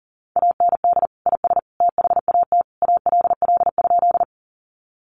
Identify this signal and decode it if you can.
Morse code, heard as a keyed tone, about 40 wpm, 700 Hz. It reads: AND IS THAT ALL?
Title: Sentences from Top 100 Words 40wpm